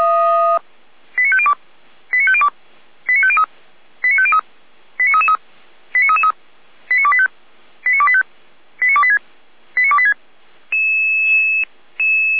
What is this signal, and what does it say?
Five-tone selective call (ZVEI), heard as quick runs of five tones, tones of about 70 ms. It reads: 89502 89502 89503 89503 89333 89333 89285 89285 89286 89286